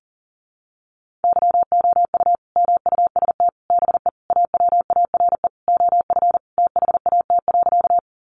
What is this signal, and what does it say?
Morse code, heard as a keyed tone, about 37 wpm, 700 Hz. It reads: YOU MUST BE AWARE OF THAT.
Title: Sentences from Top 1200 Words 40wpm